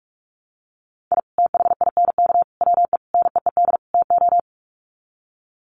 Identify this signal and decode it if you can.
Morse code, heard as a keyed tone, about 44 wpm, 700 Hz. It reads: I THINK WE NEED TO
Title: Sets of 5 Words 45wpm